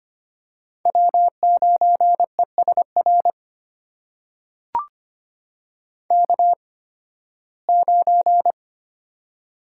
Morse code, heard as a keyed tone, about 25 wpm, spaced as for 16 wpm, 700 Hz. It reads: W9ESR K9
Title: USA Call Signs - 5 Characters - 8x Character Spacing - 25wpm